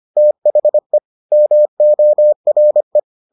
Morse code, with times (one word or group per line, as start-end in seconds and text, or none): 0.2-1.0 THE
1.3-3.0 MORE